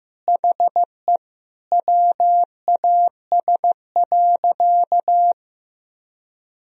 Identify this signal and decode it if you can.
Morse code, heard as a keyed tone, about 15 wpm, 700 Hz. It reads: HE WAS.
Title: Sentences from Top 1000 Words - Review Part 1 - 15wpm